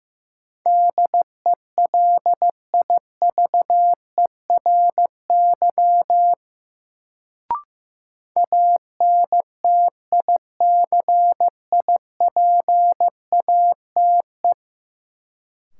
Morse code, heard as a keyed tone, about 15 wpm, 700 Hz. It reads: DELIVERY ANTICIPATE
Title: Sentences from Top 1500 Words - New Word Drill Alt - 15wpm